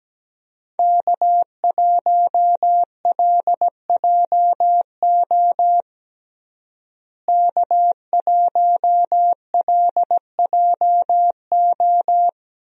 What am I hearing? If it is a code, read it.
Morse code, heard as a keyed tone, about 17 wpm, 700 Hz. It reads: K1LJO K1LJO
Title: USA Call Signs - 5 Characters - Repeated 2x - 17wpm